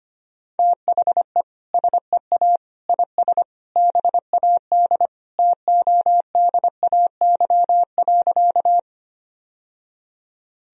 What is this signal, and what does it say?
Morse code, heard as a keyed tone, about 25 wpm, 700 Hz. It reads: THE SEA IS BAD TODAY.
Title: Sentences from Top 1000 Words - Review Part 1 - 25wpm